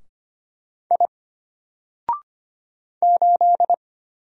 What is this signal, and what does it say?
Morse code, heard as a keyed tone, about 25 wpm, 700 Hz. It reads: I 8
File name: Single-Letter-Number-25wpm